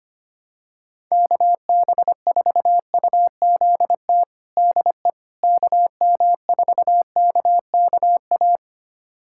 Morse code, heard as a keyed tone, about 25 wpm, 700 Hz. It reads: KB4UZT DE KM4KKA
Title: USA Callsign DE Callsign - 6 Characters - 25wpm